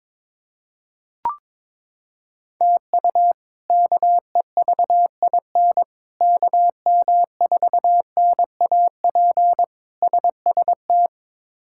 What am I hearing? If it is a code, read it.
Morse code, heard as a keyed tone, about 22 wpm, 700 Hz. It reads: TU KEVIN KM4NAP SST
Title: K1USN SST QSO Practice - 22wpm